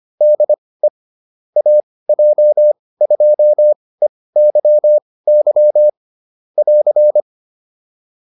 Morse code, heard as a keyed tone, about 25 wpm, spaced as for 18 wpm, 600 Hz. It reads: DE AJ2EYY <AR>
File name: QSO307 18